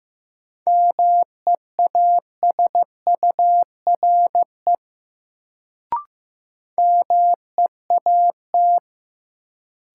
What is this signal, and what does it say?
Morse code, heard as a keyed tone, about 15 wpm, 700 Hz. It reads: MEASURE MEAT
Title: Sentences from Top 1200 Words - New Word Drill Alt - 15wpm